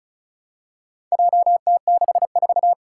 Morse code, heard as a keyed tone, about 36 wpm, 700 Hz. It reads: JT64
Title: Four Challenging Letter-Number Combinations 35wpm